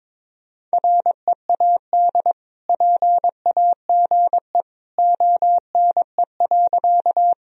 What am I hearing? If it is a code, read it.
Morse code, heard as a keyed tone, about 22 wpm, 700 Hz. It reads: READ PAGE ONE.
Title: Sentences from Top 1000 Words - Review Part 1 - 22wpm